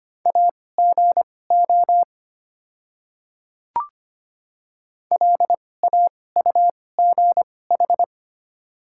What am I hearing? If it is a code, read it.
Morse code, heard as a keyed tone, about 25 wpm, spaced as for 18 wpm, 700 Hz. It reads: AGO LAUGH
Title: 2x Character Spacing - Top 500 Words - 25wpm